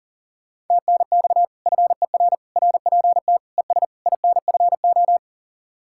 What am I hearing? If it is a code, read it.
Morse code, heard as a keyed tone, about 51 wpm, 700 Hz. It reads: TNX FER RPT ES INFO